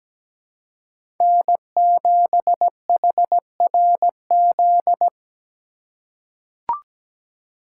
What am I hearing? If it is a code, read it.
Morse code, heard as a keyed tone, about 17 wpm, 700 Hz. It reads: N7HRZ
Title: USA Call Signs - 5 Characters - 4x Character Spacing - 17wpm